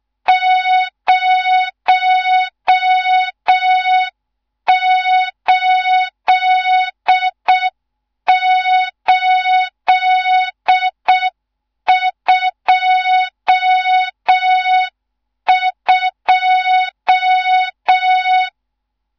Morse code, heard as a keyed tone, about 6 wpm, 3700 Hz, with noise. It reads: 08822
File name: Morze-HA2MN / hang / 14 280 HA2MN